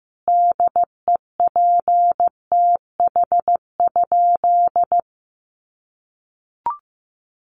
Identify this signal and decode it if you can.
Morse code, heard as a keyed tone, about 15 wpm, 700 Hz. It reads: DEPTH?
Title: Sentences from Top 1200 Words 15wpm